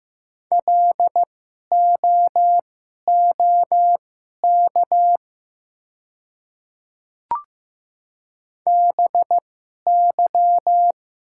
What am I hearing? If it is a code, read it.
Morse code, heard as a keyed tone, about 15 wpm, spaced as for 11 wpm, 700 Hz. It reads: LOOK BY
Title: 2x Character Spacing - Top 100 Words - 15wpm